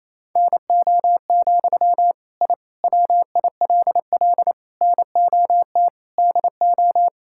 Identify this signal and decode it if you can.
Morse code, heard as a keyed tone, about 28 wpm, 700 Hz. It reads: NO, I WILL NOT DO
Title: Sentences from Top 1000 Words - Review Part 6 - 28wpm